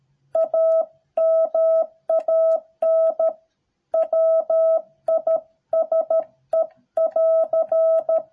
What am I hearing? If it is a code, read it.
Morse code, heard as a keyed tone, about 13 wpm, 650 Hz. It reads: A MAN WISE<AR>